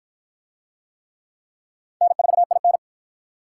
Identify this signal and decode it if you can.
Morse code, heard as a keyed tone, about 53 wpm, 700 Hz. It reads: N4IN